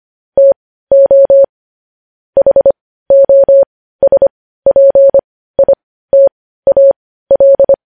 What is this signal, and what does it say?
Morse code, heard as a keyed tone, about 25 wpm, spaced as for 15 wpm, 550 Hz. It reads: TO HOSPITAL
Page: news headlines in morse code at 15 wpm